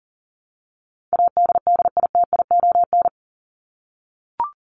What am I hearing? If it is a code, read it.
Morse code, heard as a keyed tone, about 40 wpm, 700 Hz. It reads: ADDITION
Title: Top 751-1000 Words 40wpm